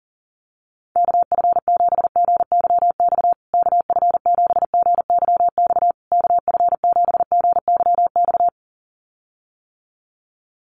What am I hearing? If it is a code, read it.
Morse code, heard as a keyed tone, about 40 wpm, 700 Hz. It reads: KF7GYX KF7GYX KF7GYX
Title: USA Call Signs Repeated 3x 40wpm